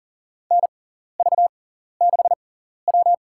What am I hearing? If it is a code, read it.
Morse code, heard as a keyed tone, about 40 wpm, spaced as for 14 wpm, 700 Hz. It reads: NV6W
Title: USA Call Signs - 4 Characters - 6x Character Spacing - 40wpm